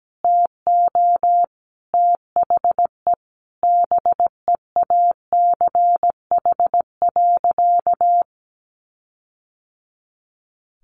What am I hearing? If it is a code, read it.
Morse code, heard as a keyed tone, about 17 wpm, 700 Hz. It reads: TO THE BEACH.
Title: Sentences from Top 1500 Words 17wpm